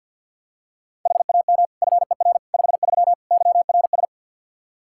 Morse code, heard as a keyed tone, about 51 wpm, 700 Hz. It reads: HAM FER 53 YRS
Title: Practice QSO - Exchange 3 - More Info - 50wpm